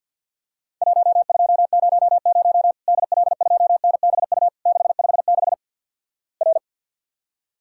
Morse code, heard as a keyed tone, about 51 wpm, 700 Hz. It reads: 1200 DL2NBU 656 R
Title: Worked All Europe DX Contest - QTC Exchange - 50wpm